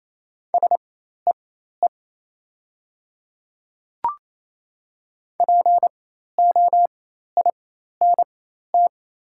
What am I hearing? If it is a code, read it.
Morse code, heard as a keyed tone, about 27 wpm, spaced as for 13 wpm, 700 Hz. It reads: SEE POINT
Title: 4x Character Spacing - Top 300 Words - 28wpm